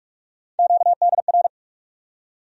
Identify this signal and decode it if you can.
Morse code, heard as a keyed tone, about 46 wpm, 700 Hz. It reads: QDR
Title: USA Call Sign Postfixes - 3 Letters 45wpm